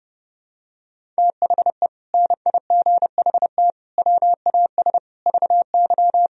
Morse code, heard as a keyed tone, about 28 wpm, 700 Hz. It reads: THE NIGHT WAS VY
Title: Sentences from Top 500 Words and Abbreviations 30wpm